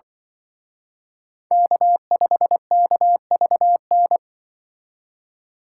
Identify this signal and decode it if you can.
Morse code, heard as a keyed tone, about 24 wpm, 700 Hz. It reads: K5KVN
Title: Speed Racing - USA Call Signs 20wpm